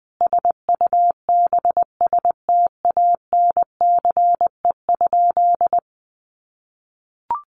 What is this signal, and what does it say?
Morse code, heard as a keyed tone, about 20 wpm, 700 Hz. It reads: SUBSTANCE?
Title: Sentences from Top 1500 Words 20wpm